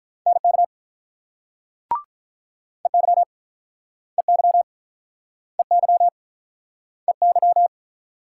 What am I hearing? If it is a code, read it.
Morse code, heard as a keyed tone, about 50 wpm, 700 Hz. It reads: NK EY EY EY EY